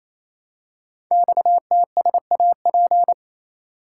Morse code, heard as a keyed tone, about 27 wpm, 700 Hz. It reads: XTSAP
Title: Code Groups - 5 Letters - 28wpm